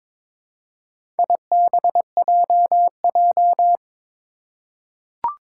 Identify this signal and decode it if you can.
Morse code, heard as a keyed tone, about 23 wpm, 700 Hz. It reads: IBJJ
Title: Four Challenging Letter-Number Combinations 22wpm